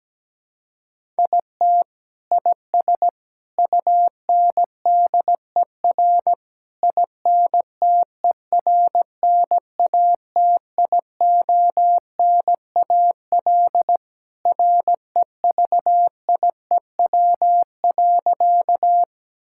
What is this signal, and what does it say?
Morse code, heard as a keyed tone, about 17 wpm, 700 Hz. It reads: IT IS UNDER INTERNATIONAL REVIEW.
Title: Sentences from Top 1000 Words - Review Part 6 - 17wpm